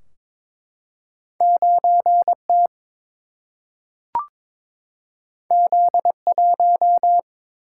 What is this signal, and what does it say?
Morse code, heard as a keyed tone, about 22 wpm, 700 Hz. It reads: 9 T Z 1
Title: Single-Letter Single-Number 22wpm